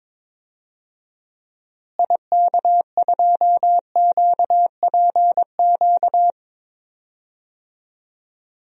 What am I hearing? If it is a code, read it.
Morse code, heard as a keyed tone, about 21 wpm, 700 Hz. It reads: IK2QPQ
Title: DXCC-100 Call Signs - 6 Characters - Repeated 2x - 22wpm